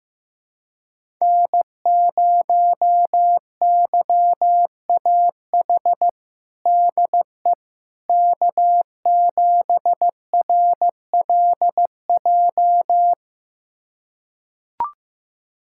Morse code, heard as a keyed tone, about 16 wpm, 700 Hz. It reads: N0YAH DE K7RLJ
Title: USA Callsign DE Callsign - 5 Characters - 15wpm